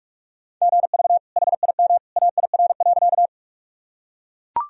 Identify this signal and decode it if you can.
Morse code, heard as a keyed tone, about 42 wpm, 700 Hz. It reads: GV HIM AIR.
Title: Sentences from Top 200 Words and Abbreviations 45wpm